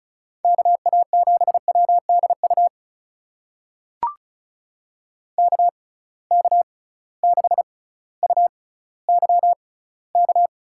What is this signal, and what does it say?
Morse code, heard as a keyed tone, about 36 wpm, spaced as for 18 wpm, 700 Hz. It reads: KA7WDU KK6UYK
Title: USA Call Signs - 6 Characters - 6x Character Spacing - 35wpm